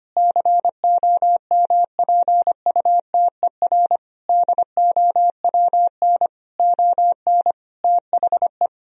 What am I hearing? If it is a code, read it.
Morse code, heard as a keyed tone, about 25 wpm, 700 Hz. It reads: COMPUTER DOWN ON THE
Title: Sentences from Top 1000 Words - Review Part 3 - 25wpm